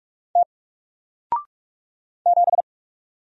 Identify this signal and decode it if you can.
Morse code, heard as a keyed tone, about 44 wpm, 700 Hz. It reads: T 7
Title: Single Letter-Number Alt 45wpm